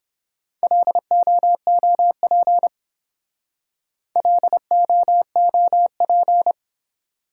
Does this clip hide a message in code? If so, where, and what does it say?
Yes: Morse code from 0.6 to 6.5 s, heard as a keyed tone, about 30 wpm, 700 Hz. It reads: LOOP LOOP